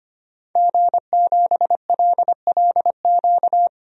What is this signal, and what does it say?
Morse code, heard as a keyed tone, about 25 wpm, 700 Hz. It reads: G7LLQ
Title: DXCC-100 Call Signs - 5 Characters - 25wpm